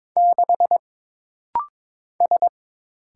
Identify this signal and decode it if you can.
Morse code, heard as a keyed tone, about 22 wpm, 700 Hz. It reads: 6 S
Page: This rapid-fire timing is geared to push your ICR (Instant Character Recognition) skills.